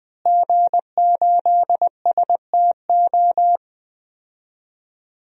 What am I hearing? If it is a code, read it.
Morse code, heard as a keyed tone, about 20 wpm, 700 Hz. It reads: G8STO
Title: DXCC-100 Call Signs - 5 Characters - Repeated 3x - 20wpm